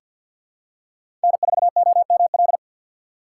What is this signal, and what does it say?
Morse code, heard as a keyed tone, about 50 wpm, 690 Hz. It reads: N4OML
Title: USA Call Signs - 5 Characters - 3x Character Spacing - 50wpm